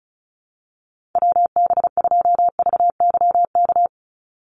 Morse code, heard as a keyed tone, about 36 wpm, 700 Hz. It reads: WB2VYK
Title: Call Signs - 35wpm